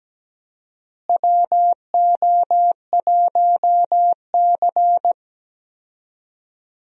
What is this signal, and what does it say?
Morse code, heard as a keyed tone, about 17 wpm, 700 Hz. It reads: WO1C
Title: USA Call Signs - 4 Characters - Repeated 3x - 17wpm